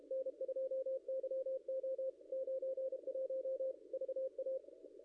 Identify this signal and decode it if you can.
Morse code, heard as a keyed tone, about 29 wpm, 500 Hz. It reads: N2YO 91 VA
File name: contests / ss-cw-k5tr-op-n5rz -0700